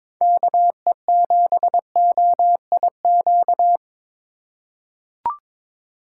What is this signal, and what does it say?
Morse code, heard as a keyed tone, about 21 wpm, 700 Hz. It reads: KE7OIQ
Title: USA Call Signs - 6 Characters - 4x Character Spacing - 22wpm